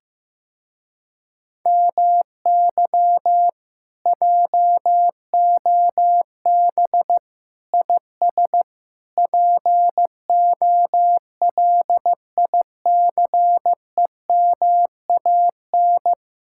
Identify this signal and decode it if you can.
Morse code, heard as a keyed tone, about 15 wpm, 700 Hz. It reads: MY JOB IS POLICEMAN
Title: Speed-Racing Common QSO Phrases Encore - 15wpm